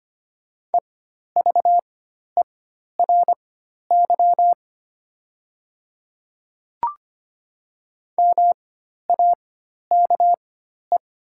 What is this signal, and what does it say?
Morse code, heard as a keyed tone, about 25 wpm, spaced as for 12 wpm, 700 Hz. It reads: EVERY MAKE
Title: 4x Character Spacing - Top 300 Words - 25wpm